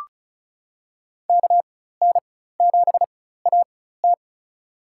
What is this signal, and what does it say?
Morse code, heard as a keyed tone, about 36 wpm, spaced as for 16 wpm, 700 Hz. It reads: KN7AT
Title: USA Call Signs - 5 Characters - 4x Character Spacing - 35wpm